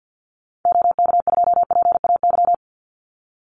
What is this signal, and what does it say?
Morse code, heard as a keyed tone, about 49 wpm, 700 Hz. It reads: OK2PAY
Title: Call Signs Repeated 3x - 50wpm